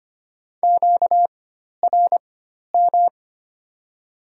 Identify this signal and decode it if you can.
Morse code, heard as a keyed tone, about 25 wpm, spaced as for 12 wpm, 700 Hz. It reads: QRM
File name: Top 90 QSO Elements - 4x Character Spacing - 25wpm